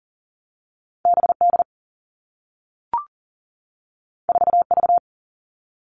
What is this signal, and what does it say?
Morse code, heard as a keyed tone, about 41 wpm, 700 Hz. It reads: BD 4V
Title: All Cut Numbers - 2-Digit 40wpm